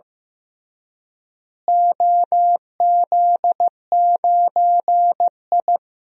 Morse code, heard as a keyed tone, about 15 wpm, 700 Hz. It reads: OZ9I
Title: DXCC-100 Call Signs - 4 Characters - Repeated 3x - 15wpm